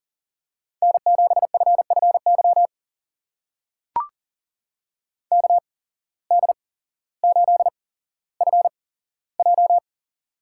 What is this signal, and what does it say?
Morse code, heard as a keyed tone, about 40 wpm, spaced as for 17 wpm, 700 Hz. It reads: N7FFY KD8FJ
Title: USA Call Signs - 8x Character Spacing - 40wpm